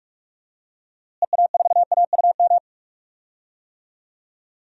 Morse code, heard as a keyed tone, about 46 wpm, 700 Hz. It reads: EA4AUM
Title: DXCC-100 Call Signs - 6 Characters - Repeated 3x - 45wpm